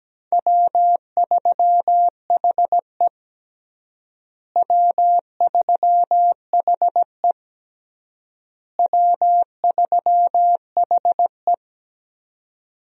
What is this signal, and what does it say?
Morse code, heard as a keyed tone, about 17 wpm, 700 Hz. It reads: W3HE W3HE W3HE
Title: USA Call Signs - 4 Characters - Repeated 3x - 17wpm